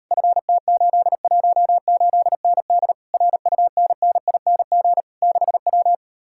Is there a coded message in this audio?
Yes: Morse code from 0.1 to 5.9 s, heard as a keyed tone, about 39 wpm, 700 Hz. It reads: FT818ND RUNNING 6W